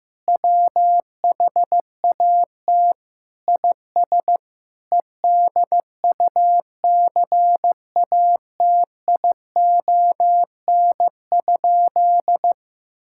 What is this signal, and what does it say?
Morse code, heard as a keyed tone, about 15 wpm, 700 Hz. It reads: WHAT IS EDUCATION?
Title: Sentences from Top 1600 Words 15wpm